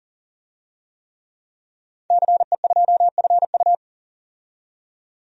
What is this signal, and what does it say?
Morse code, heard as a keyed tone, about 40 wpm, 700 Hz. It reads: CE2FU
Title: DXCC-100 Call Signs - 5 Characters - 40wpm